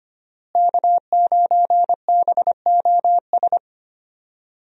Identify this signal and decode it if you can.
Morse code, heard as a keyed tone, about 25 wpm, 700 Hz. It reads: K9BOS
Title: USA Call Signs - 5 Characters - Repeated 3x - 25wpm